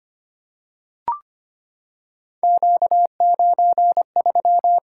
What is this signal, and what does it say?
Morse code, heard as a keyed tone, about 25 wpm, 700 Hz. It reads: Q93
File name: Three Letter-Number Combinations - 25wpm